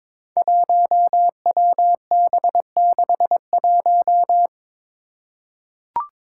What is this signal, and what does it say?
Morse code, heard as a keyed tone, about 22 wpm, 700 Hz. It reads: 1WB61
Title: Five Challenging Letter-Number Combinations 22wpm